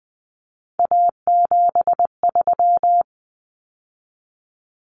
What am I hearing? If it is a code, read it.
Morse code, heard as a keyed tone, about 20 wpm, 700 Hz. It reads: A73
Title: Common Cut Numbers - 3-Digit 20wpm